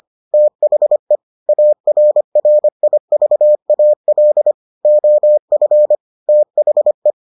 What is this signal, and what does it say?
Morse code, heard as a keyed tone, about 25 wpm, 600 Hz. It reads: THE ARRIVAL OF THE